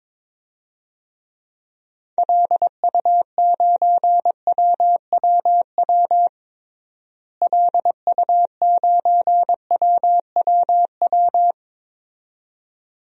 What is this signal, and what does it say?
Morse code, heard as a keyed tone, about 22 wpm, 700 Hz. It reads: LU9WWW LU9WWW
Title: DXCC-100 Call Signs - 6 Characters - Repeated 2x - 22wpm